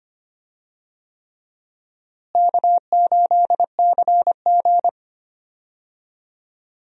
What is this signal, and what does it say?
Morse code, heard as a keyed tone, about 25 wpm, 700 Hz. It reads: K8CG
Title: USA Call Signs - 4 Characters - Repeated 3x - 25wpm